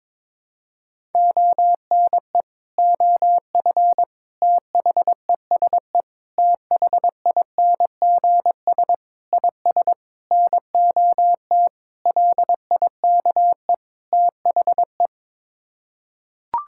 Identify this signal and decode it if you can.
Morse code, heard as a keyed tone, about 22 wpm, 700 Hz. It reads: ONE OF THESE THINGS IS NOT LIKE THE
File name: practice / 22wpm / Sets of 8 Words - 22wpm